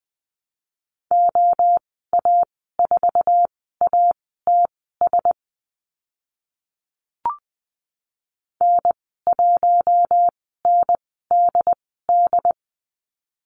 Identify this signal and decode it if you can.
Morse code, heard as a keyed tone, about 20 wpm, spaced as for 15 wpm, 700 Hz. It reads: OA4ATS N1NDD
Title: 069 Lesson 12g - Letter d - Call Signs 20wpm